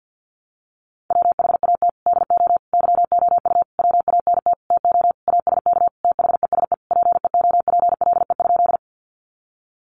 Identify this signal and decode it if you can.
Morse code, heard as a keyed tone, about 50 wpm, 700 Hz. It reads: WHAT DO YOU WANT TO ASK THESE PEOPLE?
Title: Sentences from Top 200 Words 50wpm